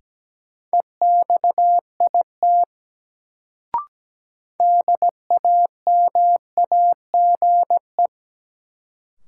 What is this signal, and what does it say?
Morse code, heard as a keyed tone, about 17 wpm, 700 Hz. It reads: EXIT DAMAGE
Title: Sentences from Top 1100 Words - New Word Drill - 17wpm